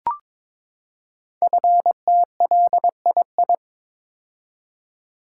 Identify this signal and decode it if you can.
Morse code, heard as a keyed tone, about 23 wpm, 700 Hz. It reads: FTLII
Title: Code Groups - 5 Letters - 22wpm